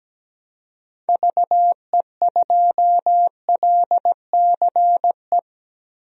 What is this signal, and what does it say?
Morse code, heard as a keyed tone, about 17 wpm, 700 Hz. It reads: VE2LCE
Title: DXCC-100 Call Signs - 6 Characters - 17wpm